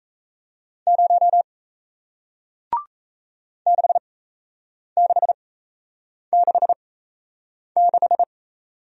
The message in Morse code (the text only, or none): 0 6666